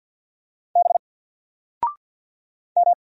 Morse code, heard as a keyed tone, about 48 wpm, 700 Hz. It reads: B M
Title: Single Letter-Number Alt 50wpm